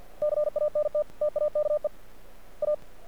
Morse code, heard as a keyed tone, about 50 wpm, 600 Hz. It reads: CANT TAKE A
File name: archived / CRSnet